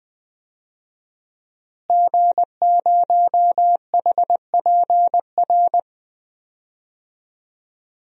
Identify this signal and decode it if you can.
Morse code, heard as a keyed tone, about 21 wpm, 700 Hz. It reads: G0HPR
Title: DXCC-100 Call Signs - 5 Characters - Repeated 3x - 20wpm